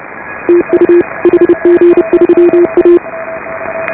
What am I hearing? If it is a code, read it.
Morse code, heard as a keyed tone, about 34 wpm, 340 Hz, with radio noise. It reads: TU HG3A
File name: AUDIO2018 / HA5BA CW